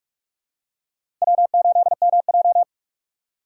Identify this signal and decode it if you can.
Morse code, heard as a keyed tone, about 46 wpm, 700 Hz. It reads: W8MJ
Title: USA Call Signs - 4 Characters - 8x Character Spacing - 45wpm